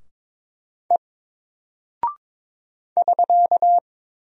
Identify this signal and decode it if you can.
Morse code, heard as a keyed tone, about 22 wpm, 700 Hz. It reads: E <SK>
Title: Single Character (Letter:Number:Punctuation:Prosign) Alt 22wpm